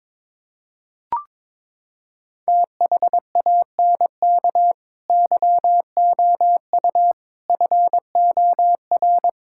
Morse code, heard as a keyed tone, about 21 wpm, 700 Hz. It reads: THANK YOU FOR
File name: Sentences with 7-Word Phrases - 22wpm